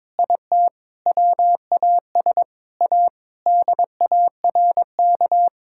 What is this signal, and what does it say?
Morse code, heard as a keyed tone, about 22 wpm, 700 Hz. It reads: IT WAS A DARK
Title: Sentences from Top 1000 Words - Review Part 5 - 22wpm